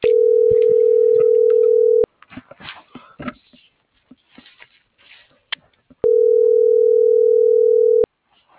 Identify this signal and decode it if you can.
Ringback tone (North America), 440 + 480 Hz.
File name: CENTRAL SAGOT / out danae.bsagot - .10031